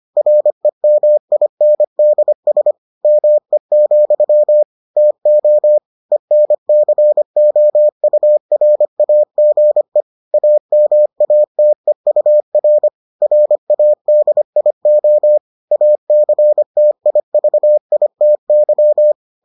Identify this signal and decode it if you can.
Morse code, heard as a keyed tone, about 25 wpm, 600 Hz. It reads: REMINDS ME, TO ENCOURAGE AMATEUR RADIO ACTIVITY